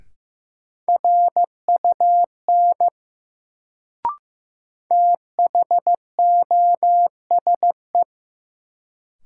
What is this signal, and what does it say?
Morse code, heard as a keyed tone, about 15 wpm, 700 Hz. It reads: RUN THOSE
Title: Sentences from Top 300 Words - New Word Drill - 15wpm